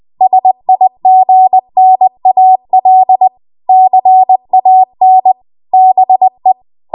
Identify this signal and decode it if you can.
Morse code, heard as a keyed tone, about 20 wpm, 750 Hz. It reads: SIGNAL CAN BE
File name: Morse / 20WPM